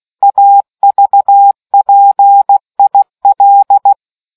Morse code, heard as a keyed tone, about 16 wpm, 800 Hz. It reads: AVPIL